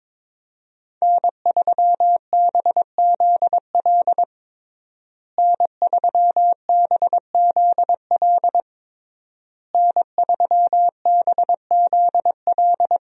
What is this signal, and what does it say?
Morse code, heard as a keyed tone, about 22 wpm, 700 Hz. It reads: N3BZL N3BZL N3BZL
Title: USA Call Signs - 5 Characters - Repeated 3x - 22wpm